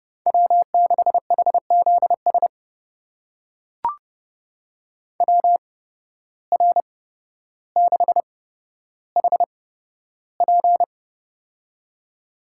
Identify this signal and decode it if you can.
Morse code, heard as a keyed tone, about 30 wpm, spaced as for 13 wpm, 700 Hz. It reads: W6HZS WR6HP
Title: USA Call Signs - 8x Character Spacing - 30wpm